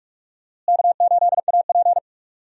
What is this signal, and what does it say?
Morse code, heard as a keyed tone, about 46 wpm, 700 Hz. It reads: K8AP
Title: USA Call Signs - 4 Characters - Repeated 2x - 45wpm